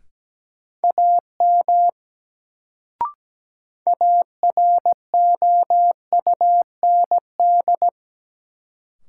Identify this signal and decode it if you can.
Morse code, heard as a keyed tone, about 17 wpm, 700 Hz. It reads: AM AROUND
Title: Sentences from Top 600 Words - New Word Drill - 17wpm